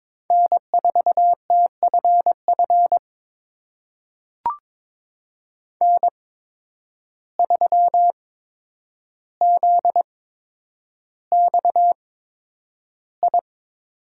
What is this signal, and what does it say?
Morse code, heard as a keyed tone, about 22 wpm, spaced as for 9 wpm, 700 Hz. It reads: N4TFF N3ZXI
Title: USA Call Signs - 5 Characters - 8x Character Spacing - 22wpm